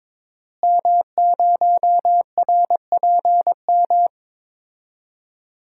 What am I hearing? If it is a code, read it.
Morse code, heard as a keyed tone, about 22 wpm, 700 Hz. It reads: M0RPM
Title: DXCC-100 Call Signs - 5 Characters - Repeated 3x - 22wpm